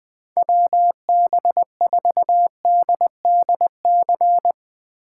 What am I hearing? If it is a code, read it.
Morse code, heard as a keyed tone, about 20 wpm, 700 Hz. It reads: WB4DDC